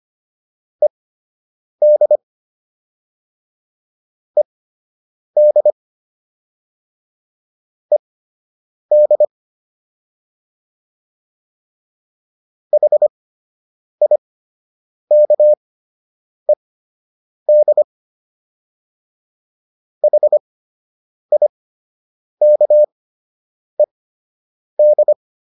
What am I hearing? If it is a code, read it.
Morse code, heard as a keyed tone, about 25 wpm, spaced as for 8 wpm, 600 Hz. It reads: ED ED ED HIKED HIKED